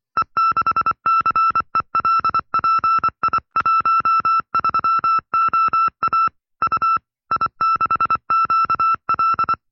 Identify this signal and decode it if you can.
Morse code, heard as a keyed tone, about 24 wpm, 1330 Hz. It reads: E6CELPI13OA U I6QL
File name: FreeFiles / CW-practice / 20WPM-side2